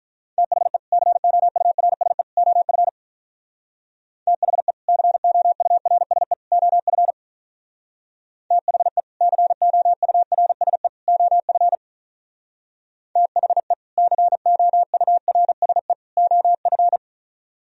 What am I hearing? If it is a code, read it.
Morse code, heard as a keyed tone, about 46 wpm, 700 Hz. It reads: THE COURSE OF THE COURSE OF THE COURSE OF THE COURSE OF